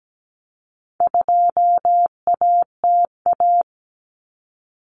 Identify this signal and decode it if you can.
Morse code, heard as a keyed tone, about 17 wpm, 700 Hz. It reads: 2ATA
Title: Common Cut Numbers - 4-Digit 17wpm